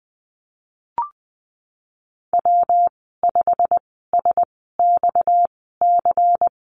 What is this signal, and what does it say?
Morse code, heard as a keyed tone, about 20 wpm, spaced as for 15 wpm, 700 Hz. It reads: W5SXC
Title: 231 Lesson 37g - Letter x - Call Signs 20wpm